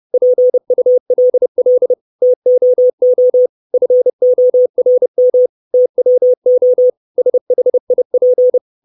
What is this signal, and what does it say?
Morse code, heard as a keyed tone, about 30 wpm, 500 Hz. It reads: PULL TOO FORM TWO SHIP
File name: 632 Words 30WPM